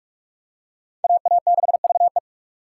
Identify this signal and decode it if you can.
Morse code, heard as a keyed tone, about 44 wpm, 700 Hz. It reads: AA6VE